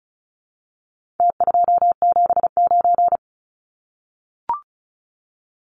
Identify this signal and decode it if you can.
Morse code, heard as a keyed tone, about 34 wpm, 700 Hz. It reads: T279
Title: Numbers - Contest Style - 4-Digit 35wpm